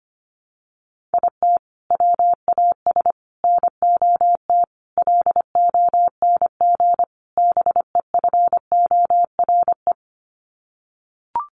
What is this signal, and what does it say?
Morse code, heard as a keyed tone, about 25 wpm, 700 Hz. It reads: IT WAS NOT LONG BEFORE
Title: Sets of 5 Words Encore2 25wpm